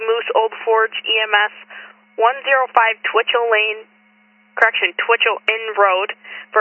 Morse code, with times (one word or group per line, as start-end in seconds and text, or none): none